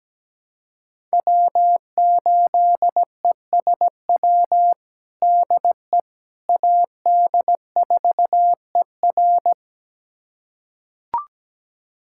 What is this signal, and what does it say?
Morse code, heard as a keyed tone, about 17 wpm, 700 Hz. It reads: W8ESW DE AD4ER
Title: USA Callsign DE Callsign - 5 Characters - 17wpm